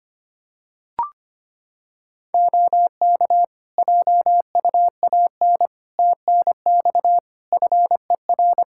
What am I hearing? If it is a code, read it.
Morse code, heard as a keyed tone, about 25 wpm, 700 Hz. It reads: OK JUAN TNX FER
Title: Practice QSO - Exchange 4 - Ending - 25wpm